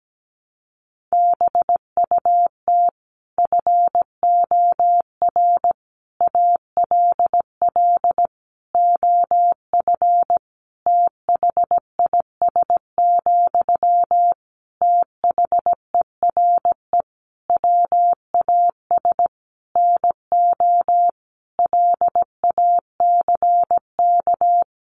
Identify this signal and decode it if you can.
Morse code, heard as a keyed tone, about 17 wpm, 700 Hz. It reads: BUT FOR ALL OF THIS, THERE WAS NO LACK
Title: Sentences from Top 1300 Words 17wpm